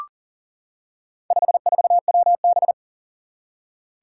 Morse code, heard as a keyed tone, about 40 wpm, 700 Hz. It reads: 54WB
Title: Four Challenging Letter-Number Combinations 40wpm